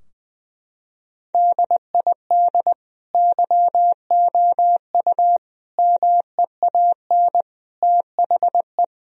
Morse code, heard as a keyed tone, about 20 wpm, 700 Hz. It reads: DID YOU MEAN THE
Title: Sentences from Top 1000 Words - Review Part 2 - 20wpm